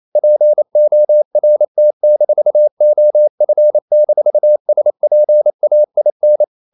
Morse code, heard as a keyed tone, about 28 wpm, 600 Hz. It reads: PORT-OF-SPAIN
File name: CW / Practice / NationalCapitals28